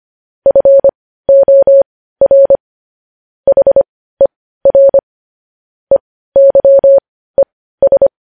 Morse code, heard as a keyed tone, about 25 wpm, spaced as for 15 wpm, 550 Hz. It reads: FOR HER EYES
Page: news-headlines-in-morse-code-at-15-wpm